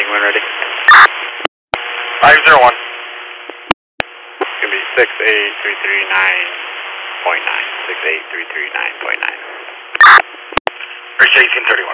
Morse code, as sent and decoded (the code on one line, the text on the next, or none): none